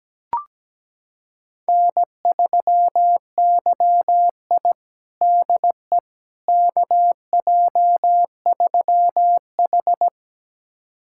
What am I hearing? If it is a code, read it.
Morse code, heard as a keyed tone, about 17 wpm, 700 Hz. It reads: N3YI DE KJ3H
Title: USA Callsign DE Callsign - 4 Characters - 17wpm